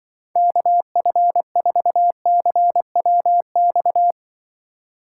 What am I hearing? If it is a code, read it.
Morse code, heard as a keyed tone, about 24 wpm, 700 Hz. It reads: KF4CWX
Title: Speed Racing - USA 2x3 Call Signs 20wpm